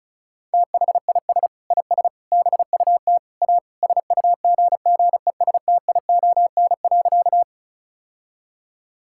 Morse code, heard as a keyed tone, about 36 wpm, 700 Hz. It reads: THIS IS BUT A SUGGESTION.
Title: Sentences from Top 1600 Words 35wpm